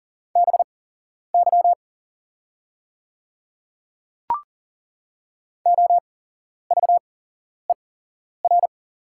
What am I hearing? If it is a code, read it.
Morse code, heard as a keyed tone, about 39 wpm, spaced as for 11 wpm, 690 Hz. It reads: BY OVER
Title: 8x Character Spacing - Top 500 Words - 40wpm